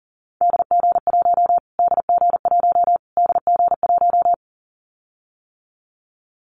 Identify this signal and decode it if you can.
Morse code, heard as a keyed tone, about 38 wpm, 700 Hz. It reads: DG1 DG1 DG1
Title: Callsign Prefixes for Top 25 DXCC - Repeated 3x - 40wpm